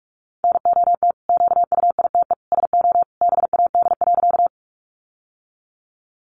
Morse code, heard as a keyed tone, about 44 wpm, 700 Hz. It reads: NOT QUITE SO BAD.
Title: Sentences from Top 700 Words 45wpm